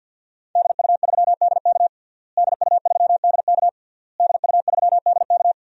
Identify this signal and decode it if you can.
Morse code, heard as a keyed tone, about 48 wpm, 700 Hz. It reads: DU3DK DU3DK DU3DK
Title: DXCC-100 Call Signs - 5 Characters - Repeated 3x - 50wpm